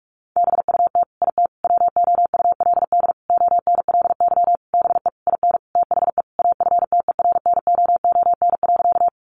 Morse code, heard as a keyed tone, about 44 wpm, 700 Hz. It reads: BUT IT WOULD ONLY BE IN THE AFTERNOON.
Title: Sentences from Top 1500 Words 45wpm